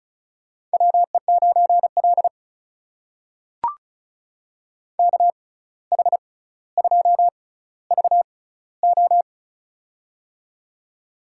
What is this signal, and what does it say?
Morse code, heard as a keyed tone, about 36 wpm, spaced as for 17 wpm, 700 Hz. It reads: WE9L KH2VO
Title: USA Call Signs - 6x Character Spacing - 35wpm